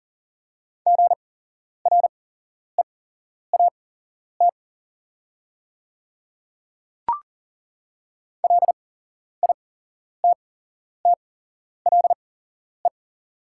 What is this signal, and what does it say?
Morse code, heard as a keyed tone, about 38 wpm, spaced as for 11 wpm, 700 Hz. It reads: GREAT LITTLE
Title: 8x Character Spacing - Top 200 Words - 40wpm